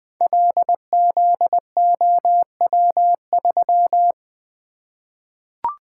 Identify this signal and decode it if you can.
Morse code, heard as a keyed tone, about 20 wpm, 700 Hz. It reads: LZOW3
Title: Code Groups - 5 Letter:Numbers - 20wpm